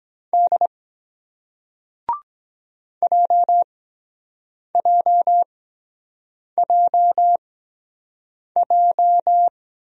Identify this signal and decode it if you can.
Morse code, heard as a keyed tone, about 23 wpm, spaced as for 7 wpm, 700 Hz. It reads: D JJJJ